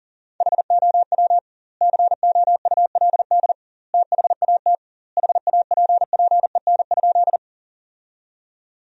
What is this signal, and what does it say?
Morse code, heard as a keyed tone, about 40 wpm, 700 Hz. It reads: HOW COULD THAT HAPPEN?
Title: 2x Word-Spacing - Sentences from Top 400 Words - 40wpm